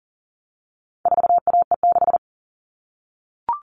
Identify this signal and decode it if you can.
Morse code, heard as a keyed tone, about 40 wpm, 700 Hz. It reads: 4AE6
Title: All Cut Numbers - 4-Digit 40wpm